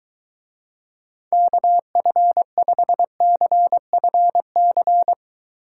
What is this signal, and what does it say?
Morse code, heard as a keyed tone, about 23 wpm, 700 Hz. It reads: KF5CFC